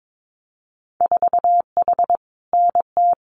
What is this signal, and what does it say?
Morse code, heard as a keyed tone, about 22 wpm, 700 Hz. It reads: 4H NT
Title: Winter Field Day Exchanges Encore 22wpm